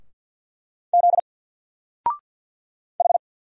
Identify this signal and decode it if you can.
Morse code, heard as a keyed tone, about 48 wpm, 700 Hz. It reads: Z H
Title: Single Letter-Number 50wpm